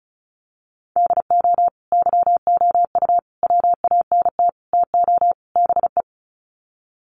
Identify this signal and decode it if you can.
Morse code, heard as a keyed tone, about 34 wpm, 700 Hz. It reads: DO YOU WANT TO BE
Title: Sets of 5 Words Encore2 35wpm